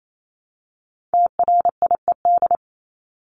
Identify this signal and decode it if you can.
Morse code, heard as a keyed tone, about 28 wpm, 700 Hz. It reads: TRIED
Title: ED Suffix - Top 100 Words with Base Words 28wpm